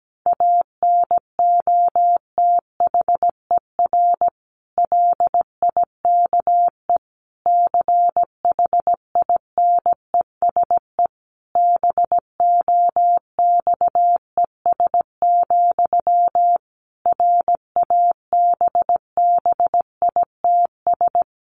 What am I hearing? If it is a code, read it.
Morse code, heard as a keyed tone, about 17 wpm, 700 Hz. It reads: ANOTHER LIKE CHINESE BOXES, RABBITS
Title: A Sound Of Thunder 17wpm